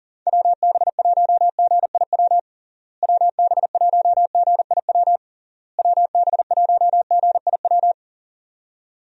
Morse code, heard as a keyed tone, about 40 wpm, 700 Hz. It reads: WB1GIW WB1GIW WB1GIW
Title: USA Call Signs - 6 Characters - Repeated 3x - 40wpm